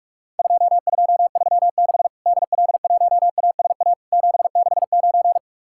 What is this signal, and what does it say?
Morse code, heard as a keyed tone, about 44 wpm, 700 Hz. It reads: 2236 DL1ASA 769
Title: Worked All Europe DX Contest - QTC Exchange - 45wpm